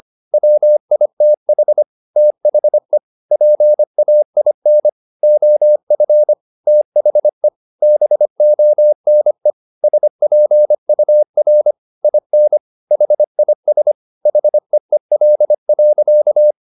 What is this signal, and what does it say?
Morse code, heard as a keyed tone, about 25 wpm, 600 Hz. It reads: WITH THE PAIN OF THE BONE SPUR IN HIS HEEL.
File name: the old man and the sea part 1 25wpm